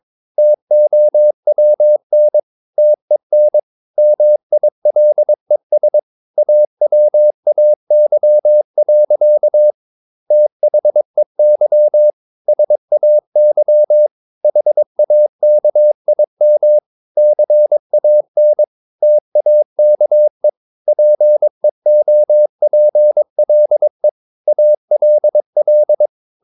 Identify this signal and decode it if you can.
Morse code, heard as a keyed tone, about 22 wpm, 600 Hz. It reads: TOWN TEN MILES AWAY. THEY SAY HAKIM CAN TAKE PEOPLE ALL
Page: Romanen Never af Ken Follet på engelsk.